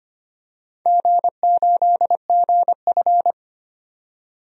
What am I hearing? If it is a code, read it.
Morse code, heard as a keyed tone, about 25 wpm, 700 Hz. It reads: G8GF
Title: DXCC-100 Call Signs - 4 Characters - 25wpm